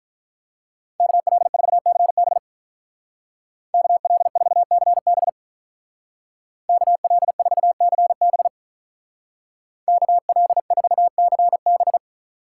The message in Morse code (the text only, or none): KL4CB KL4CB KL4CB KL4CB